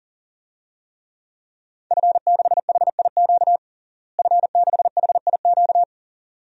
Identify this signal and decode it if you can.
Morse code, heard as a keyed tone, about 40 wpm, 700 Hz. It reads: F6HIQ F6HIQ
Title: DXCC-100 Call Signs - 5 Characters - Repeated 2x - 40wpm